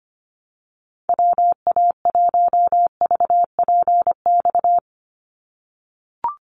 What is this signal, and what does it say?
Morse code, heard as a keyed tone, about 25 wpm, 700 Hz. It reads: WA1VPX
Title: USA Call Signs 25wpm